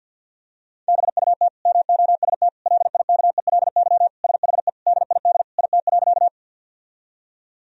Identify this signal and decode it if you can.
Morse code, heard as a keyed tone, about 50 wpm, 700 Hz. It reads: BUT MOST LIKELY SHE DID IT.
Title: Sentences from Top 1000 Words - Review Part 4 - 50wpm